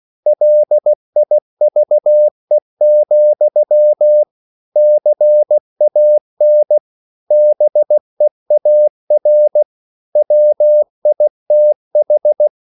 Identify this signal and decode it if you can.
Morse code, heard as a keyed tone, about 16 wpm, 600 Hz. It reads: LIVE, CAN BEAR WITH
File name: CW / Practice / EnglishQuotations0216